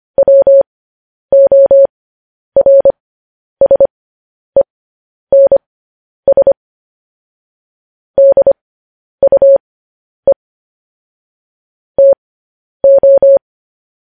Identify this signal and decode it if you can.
Morse code, heard as a keyed tone, about 25 wpm, spaced as for 10 wpm, 550 Hz. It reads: WORSENS DUE TO
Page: news headlines in morse code at 10 wpm